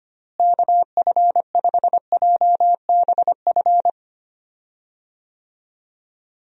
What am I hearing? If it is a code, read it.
Morse code, heard as a keyed tone, about 25 wpm, 700 Hz. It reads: KF5JBF
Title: USA Call Signs - 6 Characters - Repeated 2x - 25wpm